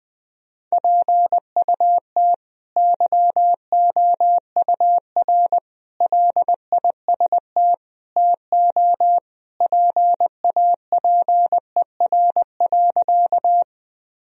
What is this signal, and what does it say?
Morse code, heard as a keyed tone, about 20 wpm, 700 Hz. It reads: PUT YOUR LIST TO PAPER.
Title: Sentences from Top 1000 Words - Review Part 2 - 20wpm